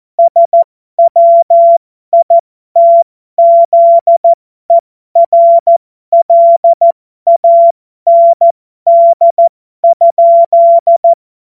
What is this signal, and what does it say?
Morse code, heard as a keyed tone, about 14 wpm, spaced as for 12 wpm, 680 Hz. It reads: SWITZERLAND?
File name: Morsekurs / Witze